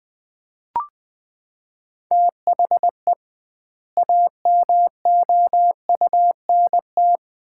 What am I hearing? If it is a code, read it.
Morse code, heard as a keyed tone, about 20 wpm, 700 Hz. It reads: THE AMOUNT
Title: 2x Word-Spacing - Sets of 3 Words - 20wpm